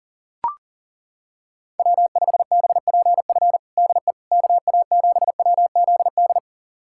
Morse code, heard as a keyed tone, about 40 wpm, 700 Hz. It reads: W5BPF DE KA7WZD
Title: USA Callsign DE Callsign 40wpm